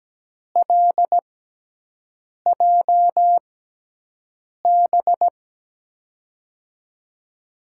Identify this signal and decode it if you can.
Morse code, heard as a keyed tone, about 17 wpm, spaced as for 6 wpm, 700 Hz. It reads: LJB
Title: USA Call Signs - 6 Characters - 6x Character Spacing - 17wpm